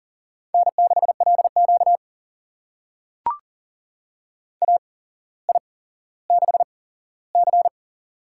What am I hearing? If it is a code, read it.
Morse code, heard as a keyed tone, about 40 wpm, spaced as for 17 wpm, 700 Hz. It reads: N6LQ AI6C
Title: USA Call Signs - 4 Characters - 8x Character Spacing - 40wpm